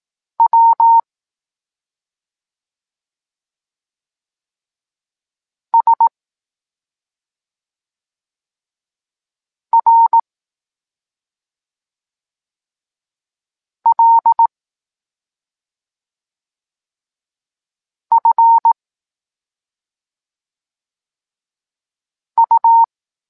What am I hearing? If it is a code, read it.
Morse code, heard as a keyed tone, about 18 wpm, 920 Hz. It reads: W S R L F U
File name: Tekst-26 1